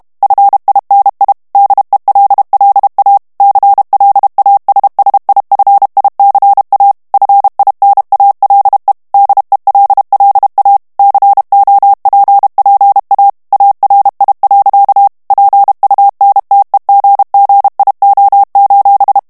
Morse code, heard as a keyed tone, about 32 wpm, 800 Hz. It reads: FINI DELLA CLASSIFICA FINALE DELLA COPPA ARI. PUNTEGGIO:
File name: ITALIEN texte clair 2 vitesse 30 mots minute